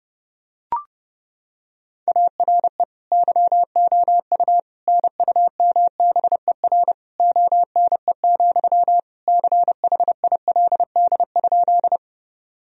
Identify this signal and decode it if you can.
Morse code, heard as a keyed tone, about 30 wpm, 700 Hz. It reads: ARE YOU NUMBER ONE, CHILD?
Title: Sentences from Top 1000 Words - Review Part 1 - 30wpm